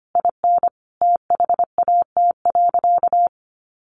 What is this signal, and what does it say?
Morse code, heard as a keyed tone, about 25 wpm, 700 Hz. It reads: IN THAT.
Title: Sentences from Top 1500 Words 25wpm